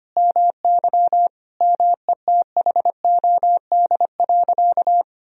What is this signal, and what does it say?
Morse code, heard as a keyed tone, about 25 wpm, 700 Hz. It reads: MY METHOD.
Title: Sentences from Top 1000 Words - Review Part 3 - 25wpm